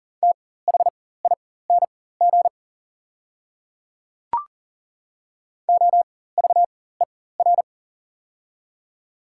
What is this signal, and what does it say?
Morse code, heard as a keyed tone, about 41 wpm, spaced as for 19 wpm, 690 Hz. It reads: THING OVER
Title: 4x Character Spacing - Top 100 Words - 40wpm